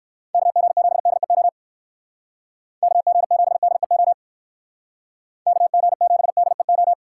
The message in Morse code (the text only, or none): KG7DEO KG7DEO KG7DEO